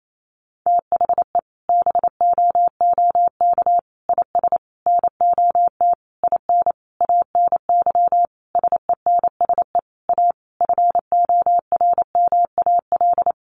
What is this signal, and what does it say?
Morse code, heard as a keyed tone, about 28 wpm, 700 Hz. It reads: THE BOOK IS NOT IN ANY SENSE A FORMAL
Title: Sentences from Top 1300 Words 28wpm